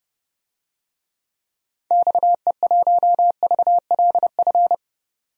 Morse code, heard as a keyed tone, about 30 wpm, 700 Hz. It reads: XE1VLF
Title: DXCC-100 Call Signs - 6 Characters - 30wpm